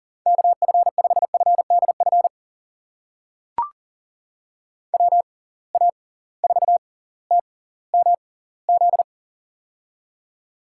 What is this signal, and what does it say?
Morse code, heard as a keyed tone, about 40 wpm, spaced as for 20 wpm, 700 Hz. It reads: KF5FDF WA4TMZ
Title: USA Call Signs - 6 Characters - 6x Character Spacing - 40wpm